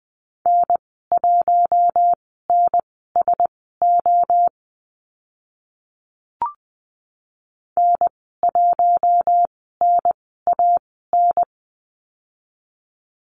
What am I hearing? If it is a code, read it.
Morse code, heard as a keyed tone, about 20 wpm, spaced as for 15 wpm, 700 Hz. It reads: N1NSO N1NAN
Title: 042 Lesson 8g - Number 1 - Call Signs 20wpm